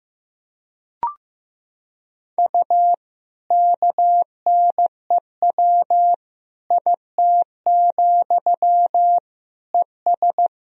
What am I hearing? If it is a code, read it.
Morse code, heard as a keyed tone, about 15 wpm, 700 Hz. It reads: U KNEW IT, ES
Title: Sentences from Top 500 Words and Abbreviations 15wpm